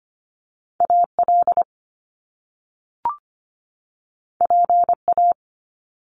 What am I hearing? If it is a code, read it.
Morse code, heard as a keyed tone, about 25 wpm, 700 Hz. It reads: AL PA
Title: Top Two Letter Combinations in Words 25wpm